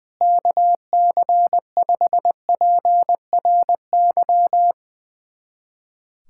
Morse code, heard as a keyed tone, about 20 wpm, 700 Hz. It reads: KC5PRY
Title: USA Callsign DE Callsign - 6 Characters - 20wpm